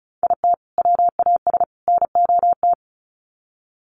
Morse code, heard as a keyed tone, about 34 wpm, 700 Hz. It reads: IT WAS NOT
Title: Sets of 3 Words 35wpm